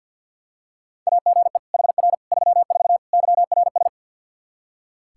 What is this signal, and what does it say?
Morse code, heard as a keyed tone, about 47 wpm, 700 Hz. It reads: AGE HR 34 YRS
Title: Practice QSO - Exchange 3 - More Info - 50wpm